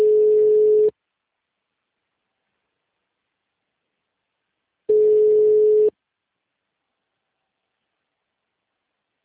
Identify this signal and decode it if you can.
Ringback tone (Europe, 425 Hz).